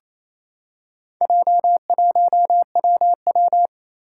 Morse code, heard as a keyed tone, about 28 wpm, 700 Hz. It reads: J1WW
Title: Four Challenging Letter-Number Combinations 28wpm